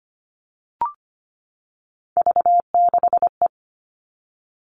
Morse code, heard as a keyed tone, about 25 wpm, 700 Hz. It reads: V6E
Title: All Cut Numbers - 3-Digit 25wpm